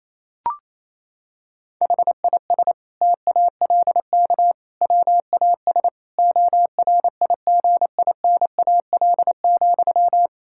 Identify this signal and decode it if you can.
Morse code, heard as a keyed tone, about 28 wpm, 700 Hz. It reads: HIS TALK WAS ORIGINAL,
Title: Sentences from Top 1000 Words - Review Part 6 - 28wpm